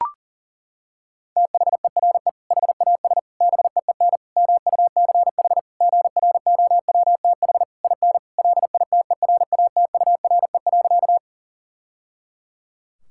Morse code, heard as a keyed tone, about 40 wpm, 700 Hz. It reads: THERE HAS BEEN MUCH GROWTH IN LITERATURE.
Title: Sentences from Top 1000 Words - Review Part 5 - 40wpm